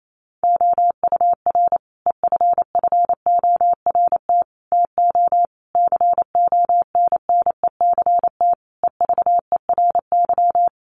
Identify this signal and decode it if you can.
Morse code, heard as a keyed tone, about 28 wpm, 700 Hz. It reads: OUR EFFORT TO CONNECT EVERY
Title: Sentences from Top 1500 Words 28wpm